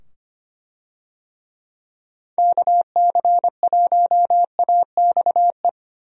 Morse code, heard as a keyed tone, about 25 wpm, 700 Hz. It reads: KC1AXE
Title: USA Call Signs - 6 Characters - 25wpm